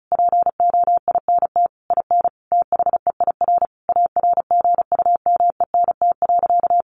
Morse code, heard as a keyed tone, about 34 wpm, 700 Hz. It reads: POINT IN THEIR ARGUMENT.
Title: Sentences from Top 1500 Words 35wpm